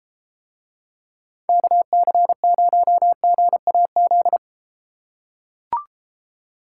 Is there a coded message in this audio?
Yes: Morse code from 1.5 to 4.4 s, heard as a keyed tone, about 33 wpm, 700 Hz. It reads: KC0GAZ